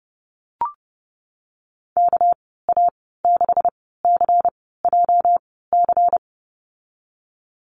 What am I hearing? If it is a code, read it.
Morse code, heard as a keyed tone, about 30 wpm, spaced as for 17 wpm, 700 Hz. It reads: KA6CJC
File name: Lesson 37g - Letter x - Call Signs 30wpm